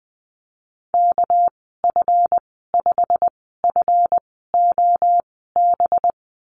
Morse code, heard as a keyed tone, about 20 wpm, spaced as for 14 wpm, 700 Hz. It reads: KF5FOB
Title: 213 Lesson 34g - Number 8 - Call Signs 20wpm